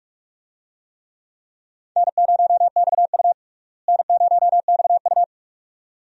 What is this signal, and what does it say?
Morse code, heard as a keyed tone, about 42 wpm, 700 Hz. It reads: N0XU N0XU
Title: USA Call Signs - 4 Characters - Repeated 2x - 45wpm